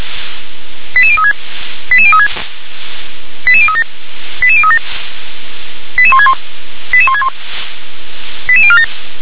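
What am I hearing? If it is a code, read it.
Five-tone selective call (ZVEI), heard as quick runs of five tones, tones of about 70 ms. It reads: 88036 88036 88037 88037 88161 88161 88047
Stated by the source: ZVEI five-tone selective call 88182 received by SDR.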